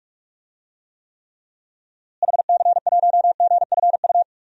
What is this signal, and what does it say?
Morse code, heard as a keyed tone, about 46 wpm, 700 Hz. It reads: HC1GFU
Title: DXCC-100 Call Signs - 6 Characters - Repeated 3x - 45wpm